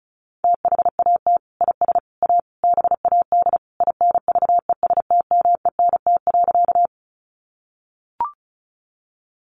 Morse code, heard as a keyed tone, about 34 wpm, 700 Hz. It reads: THAT IS A BAD INVESTMENT.
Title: Sentences from Top 700 Words 35wpm